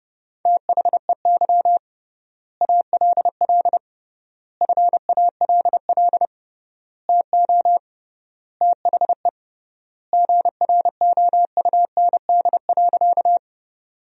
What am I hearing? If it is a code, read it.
Morse code, heard as a keyed tone, about 30 wpm, 700 Hz. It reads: THEY ALL FALL TO THE GROUND.
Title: 3x Word-Spacing - Sentences from Top 500 Words - 30wpm